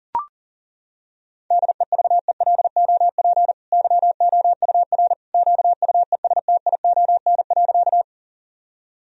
Morse code, heard as a keyed tone, about 40 wpm, 700 Hz. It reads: DEVELOP YOUR QUESTION.
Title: Sentences from Top 1000 Words - Review Part 2 - 40wpm